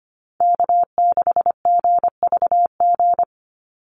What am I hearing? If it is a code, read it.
Morse code, heard as a keyed tone, about 25 wpm, 700 Hz. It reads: K6GVG
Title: Call Signs Repeated 3x - 25wpm